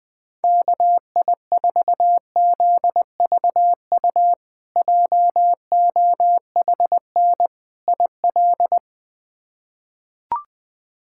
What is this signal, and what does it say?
Morse code, heard as a keyed tone, about 20 wpm, 700 Hz. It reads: KI4ZVU JOHN IL
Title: K1USN SST QSO Practice - 20wpm